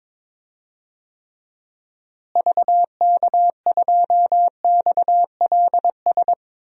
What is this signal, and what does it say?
Morse code, heard as a keyed tone, about 22 wpm, 700 Hz. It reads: VK2XLS
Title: DXCC-100 Call Signs - 6 Characters - Repeated 3x - 22wpm